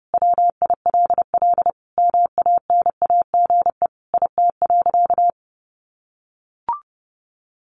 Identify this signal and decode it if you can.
Morse code, heard as a keyed tone, about 30 wpm, 700 Hz. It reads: WILL MANAGE IT.
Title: Sentences from Top 1400 Words 30wpm